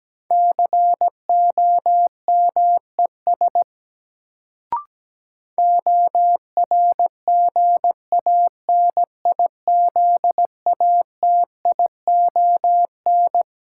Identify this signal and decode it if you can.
Morse code, heard as a keyed tone, about 17 wpm, 700 Hz. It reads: COMES ORGANIZATION
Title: Sentences from Top 1000 Words - Review Part 5 - New Word Drill - 17wpm